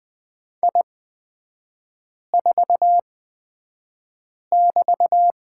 Morse code, heard as a keyed tone, about 20 wpm, spaced as for 5 wpm, 700 Hz. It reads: I4<BT>
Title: 224 Lesson 36e - BT - Three Characters 20wpm